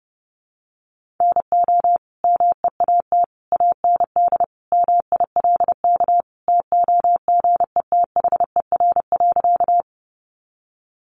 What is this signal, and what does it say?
Morse code, heard as a keyed tone, about 30 wpm, 700 Hz. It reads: NO MEAT AND MILK TOGETHER.
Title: Sentences from Top 1500 Words 30wpm